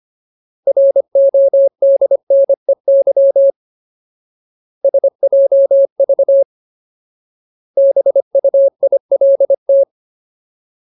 Morse code, heard as a keyed tone, about 25 wpm, 550 Hz. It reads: RODNEY SJV BUILT
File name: Ultimate Word List / 25WPM Ultimate